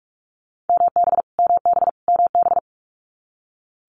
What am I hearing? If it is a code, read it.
Morse code, heard as a keyed tone, about 46 wpm, 700 Hz. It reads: MB MB MB
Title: Multi-letter Phonemes Repeated 3x 45wpm